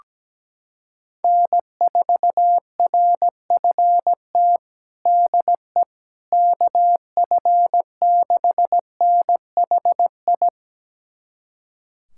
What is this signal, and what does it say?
Morse code, heard as a keyed tone, about 17 wpm, 700 Hz. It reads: N4RFT DE KF6NHI
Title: USA Callsign DE Callsign 17wpm